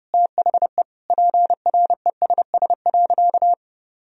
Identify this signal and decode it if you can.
Morse code, heard as a keyed tone, about 32 wpm, 700 Hz. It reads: THE PRESS.
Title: Sentences from Top 1000 Words - Review Part 1 - 30wpm